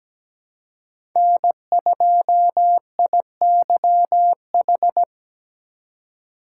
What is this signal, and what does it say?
Morse code, heard as a keyed tone, about 17 wpm, 700 Hz. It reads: N2IYH
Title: USA Call Signs - 5 Characters - 6x Character Spacing - 17wpm